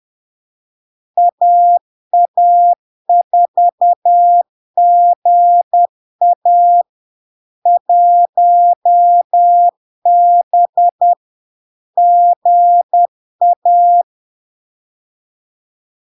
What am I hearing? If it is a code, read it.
Morse code, heard as a keyed tone, about 10 wpm, 700 Hz. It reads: AA4GA 1B GA
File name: LARK-FD-2022-Log-in-CW-10wpm-700-Hz